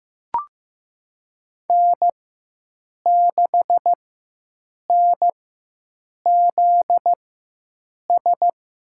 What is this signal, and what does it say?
Morse code, heard as a keyed tone, about 15 wpm, spaced as for 7 wpm, 700 Hz. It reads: N6NZS
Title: USA Call Signs - 4x Character Spacing - 15wpm